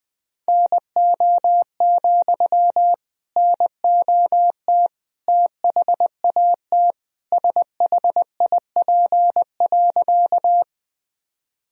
Morse code, heard as a keyed tone, about 20 wpm, 700 Hz. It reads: NO, NOT THAT SHIP.
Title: Sentences from Top 1000 Words - Review Part 2 - 20wpm